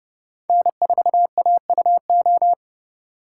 Morse code, heard as a keyed tone, about 32 wpm, 700 Hz. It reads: N4AUO
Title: USA Call Signs - 4x Character Spacing - 30wpm